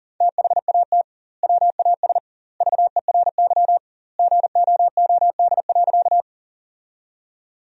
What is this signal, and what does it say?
Morse code, heard as a keyed tone, about 40 wpm, 700 Hz. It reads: THAT WAS VERY GOOD.
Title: 2x Word-Spacing - Sentences from Top 200 Words - 40wpm